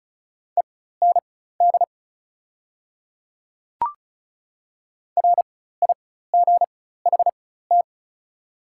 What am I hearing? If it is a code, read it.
Morse code, heard as a keyed tone, about 36 wpm, spaced as for 16 wpm, 690 Hz. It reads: END RIGHT
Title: 4x Character Spacing - Top 200 Words - 35wpm